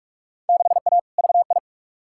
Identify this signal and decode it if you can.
Morse code, heard as a keyed tone, about 45 wpm, 700 Hz. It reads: 6A VI
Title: Field Day Exchanges Encore - 45wpm